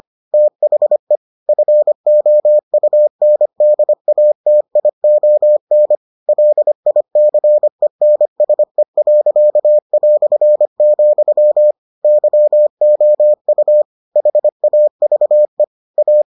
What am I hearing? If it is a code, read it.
Morse code, heard as a keyed tone, about 25 wpm, 600 Hz. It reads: THE FOUNDATION LICENSE.", YOU HAVE A